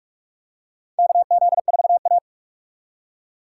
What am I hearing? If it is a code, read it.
Morse code, heard as a keyed tone, about 46 wpm, 700 Hz. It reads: KZ4A
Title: K1USN SST QSO Practice - 45wpm